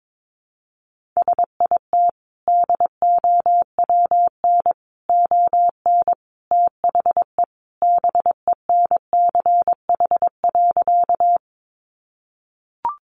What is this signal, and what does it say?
Morse code, heard as a keyed tone, about 22 wpm, 700 Hz. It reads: SIT DOWN ON THE BENCH.
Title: Sentences from Top 1400 Words 22wpm